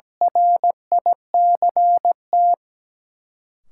Morse code, heard as a keyed tone, about 17 wpm, 700 Hz. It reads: RICT
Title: Sentences from Top 1600 Words - New Word Drill Alt - 17wpm